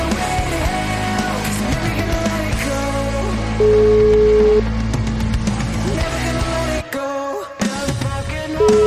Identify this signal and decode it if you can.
Ringback tone (Europe, 425 Hz), with noise.